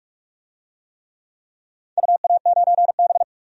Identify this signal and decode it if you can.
Morse code, heard as a keyed tone, about 43 wpm, 700 Hz. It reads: UA9B
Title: DXCC-100 Call Signs - 4 Characters - 45wpm